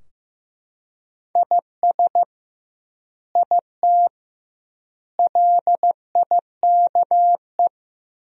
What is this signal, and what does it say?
Morse code, heard as a keyed tone, about 15 wpm, 700 Hz. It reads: IS IT LIKE
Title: 2x Word-Spacing - Sentences from Top 100 Words - 15wpm